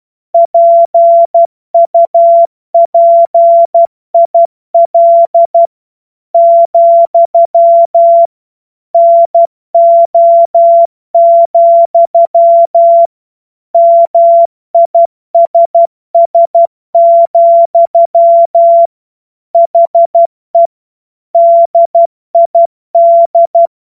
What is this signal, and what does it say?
Morse code, heard as a keyed tone, about 12 wpm, 680 Hz. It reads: PUPIL, NO, MISS, HE DID
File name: Morsekurs / Witze